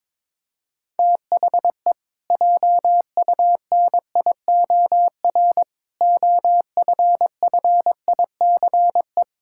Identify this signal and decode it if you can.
Morse code, heard as a keyed tone, about 22 wpm, 700 Hz. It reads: THE JUNIOR OFFICE
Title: Pangram Challenge v2 - 22wpm